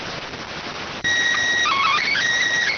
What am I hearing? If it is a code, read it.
SSTV (slow-scan television), PD 290.